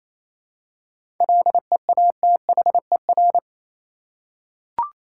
Morse code, heard as a keyed tone, about 27 wpm, 700 Hz. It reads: LEATHER
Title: Seven Letter Words - 28wpm